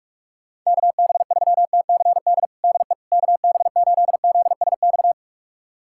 Morse code, heard as a keyed tone, about 46 wpm, 700 Hz. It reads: KB3TCD DE KB8ZSX
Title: USA Callsign DE Callsign - 6 Characters - 45wpm